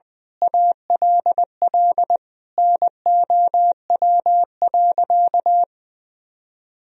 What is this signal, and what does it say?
Morse code, heard as a keyed tone, about 21 wpm, 700 Hz. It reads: ALL NOW.
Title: Sentences from Top 1000 Words - Review Part 1 - 20wpm